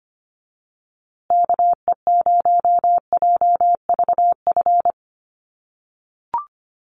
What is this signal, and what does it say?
Morse code, heard as a keyed tone, about 25 wpm, 700 Hz. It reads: KE0JVF
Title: USA Call Signs Repeated 2x 25wpm